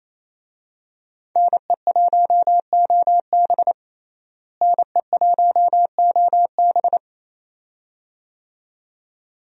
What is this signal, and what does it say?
Morse code, heard as a keyed tone, about 28 wpm, 700 Hz. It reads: NE1OB NE1OB
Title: USA Call Signs - 5 Characters - Repeated 2x - 28wpm